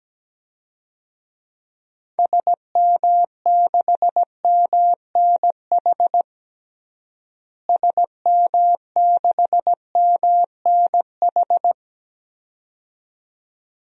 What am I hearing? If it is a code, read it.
Morse code, heard as a keyed tone, about 17 wpm, 700 Hz. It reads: SM6MNH SM6MNH
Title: DXCC-100 Call Signs - 6 Characters - Repeated 2x - 17wpm